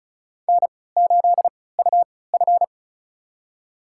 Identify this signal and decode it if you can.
Morse code, heard as a keyed tone, about 36 wpm, spaced as for 20 wpm, 700 Hz. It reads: N8UF
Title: USA Call Signs - 3x Character Spacing - 35wpm